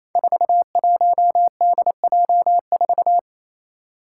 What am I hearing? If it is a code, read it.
Morse code, heard as a keyed tone, about 28 wpm, 700 Hz. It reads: 41DJ4
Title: Five Challenging Letter-Number Combinations 28wpm